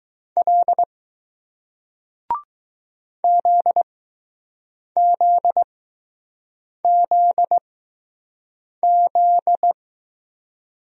Morse code, heard as a keyed tone, about 20 wpm, spaced as for 6 wpm, 700 Hz. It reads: L ZZZZ